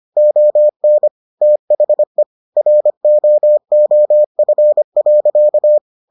Morse code, heard as a keyed tone, about 25 wpm, 600 Hz. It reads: ON THE ROOF.